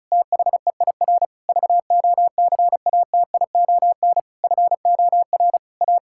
Morse code, heard as a keyed tone, about 36 wpm, 700 Hz. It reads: THEIR VOCATION FOR A
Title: Pangram Challenge v2 - 35wpm